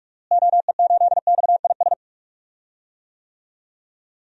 Morse code, heard as a keyed tone, about 46 wpm, 700 Hz. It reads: OE8XIS
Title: DXCC-100 Call Signs - 6 Characters - Repeated 3x - 45wpm